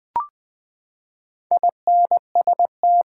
Morse code, heard as a keyed tone, about 20 wpm, 700 Hz. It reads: INST
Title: Build A Word - 4 Letters At A Time - 12-Letter Words - 20wpm